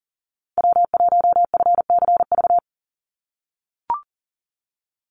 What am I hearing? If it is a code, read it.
Morse code, heard as a keyed tone, about 40 wpm, 700 Hz. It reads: W1FCV
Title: Call Signs - 40wpm